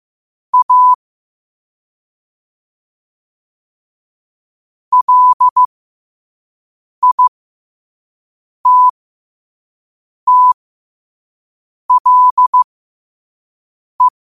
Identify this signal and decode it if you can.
Morse code, heard as a keyed tone, about 15 wpm, spaced as for 5 wpm, 1000 Hz. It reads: A LITTLE
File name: qotdfiles / 5wpm